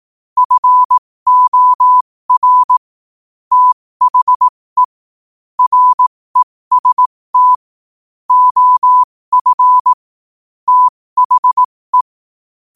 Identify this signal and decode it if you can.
Morse code, heard as a keyed tone, about 18 wpm, spaced as for 15 wpm, 1000 Hz. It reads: FOR THE REST OF THE
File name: qotdfiles / 15wpm